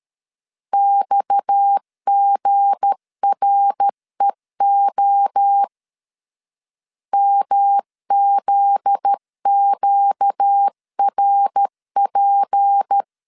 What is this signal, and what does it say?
Morse code, heard as a keyed tone, about 13 wpm, 800 Hz. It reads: XGREO MZQRP
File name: LEZIONI / 7 GF9Z / 7 GF9Z 12wpm 2147